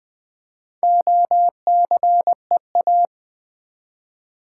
Morse code, heard as a keyed tone, about 20 wpm, 700 Hz. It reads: OCEA
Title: Build A Word - 1 Letter At A Time - 3 to 5-Letter Words - 20wpm